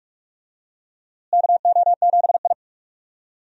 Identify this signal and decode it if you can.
Morse code, heard as a keyed tone, about 41 wpm, 700 Hz. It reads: KO7I
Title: USA Call Signs - 4 Characters - Repeated 2x - 45wpm